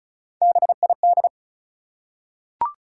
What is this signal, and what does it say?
Morse code, heard as a keyed tone, about 36 wpm, 700 Hz. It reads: BID
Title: Three Letter Words - 35wpm